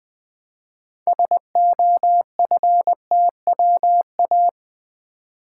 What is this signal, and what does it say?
Morse code, heard as a keyed tone, about 20 wpm, 700 Hz. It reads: SOFTWA
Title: Build A Word - 2 Letters At A Time - 8-Letter Words - 20wpm